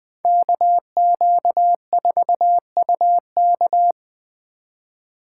Morse code, heard as a keyed tone, about 20 wpm, 700 Hz. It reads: KQ4UK
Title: Speed Racing - USA 2x2 Call Signs 17wpm